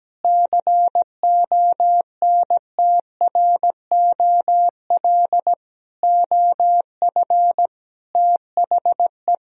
Morse code, heard as a keyed tone, about 17 wpm, 700 Hz. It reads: CONTROL OF THE
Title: Sentences from Top 1000 Words - Review Part 3 - 17wpm